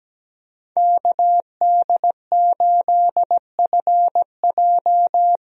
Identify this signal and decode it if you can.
Morse code, heard as a keyed tone, about 17 wpm, 700 Hz. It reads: KD8FJ
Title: USA Call Signs - 5 Characters - 17wpm